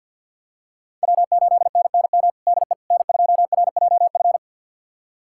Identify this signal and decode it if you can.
Morse code, heard as a keyed tone, about 52 wpm, 700 Hz. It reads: W8NNM DE N2RJF
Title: USA Callsign DE Callsign - 5 Characters - 50wpm